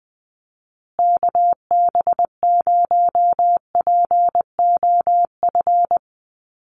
Morse code, heard as a keyed tone, about 20 wpm, 700 Hz. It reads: KB0POF
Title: USA Call Signs Repeated 2x 20wpm